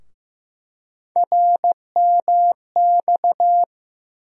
Morse code, heard as a keyed tone, about 15 wpm, 700 Hz. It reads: RMX
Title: USA Call Sign Postfixes - 3 Letters 15wpm